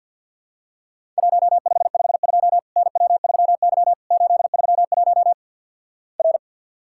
Morse code, heard as a keyed tone, about 50 wpm, 700 Hz. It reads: 1552 NW3Y 831 R
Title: Worked All Europe DX Contest - QTC Exchange - 50wpm